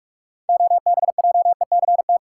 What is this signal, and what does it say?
Morse code, heard as a keyed tone, about 46 wpm, 700 Hz. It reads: OBJECT
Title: Six Letter Words - 45wpm